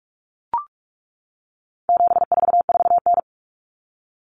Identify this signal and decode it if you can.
Morse code, heard as a keyed tone, about 44 wpm, 700 Hz. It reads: 744N
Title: Common Cut Numbers - 4-Digit 45wpm